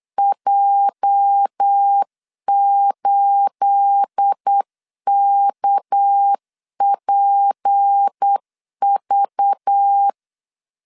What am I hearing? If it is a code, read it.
Morse code, heard as a keyed tone, about 8 wpm, 800 Hz. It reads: J8KPV
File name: LEZIONI / 9 28 8wpm 3512